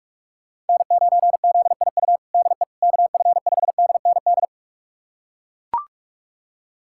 Morse code, heard as a keyed tone, about 41 wpm, 700 Hz. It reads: N9ZIU DE KF5DND
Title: USA Callsign DE Callsign 45wpm